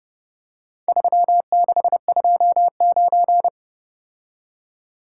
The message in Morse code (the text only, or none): 3629